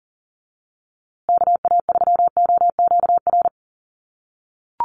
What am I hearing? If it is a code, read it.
Morse code, heard as a keyed tone, about 39 wpm, 700 Hz. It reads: KA3OQR
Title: USA Call Signs Repeated 3x 40wpm